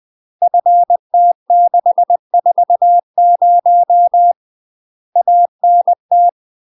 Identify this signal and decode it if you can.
Morse code, heard as a keyed tone, about 20 wpm, 700 Hz. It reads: FT640 ANT
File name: UK MASTER QSOs TRAINING / 20WPM / UK MASTER QSOs 0022